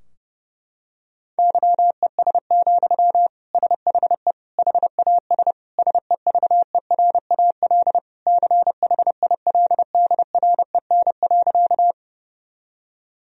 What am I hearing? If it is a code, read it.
Morse code, heard as a keyed tone, about 30 wpm, 700 Hz. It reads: YES, SHE HAS SEVERAL CHILDREN.
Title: Sentences from Top 1000 Words - Review Part 2 - 30wpm